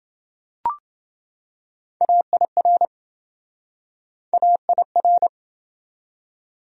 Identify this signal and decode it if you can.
Morse code, heard as a keyed tone, about 27 wpm, 700 Hz. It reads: AIR AIR